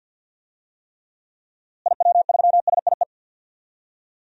Morse code, heard as a keyed tone, about 50 wpm, 700 Hz. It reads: IW3SIE
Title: DXCC-100 Call Signs - 6 Characters - Repeated 3x - 50wpm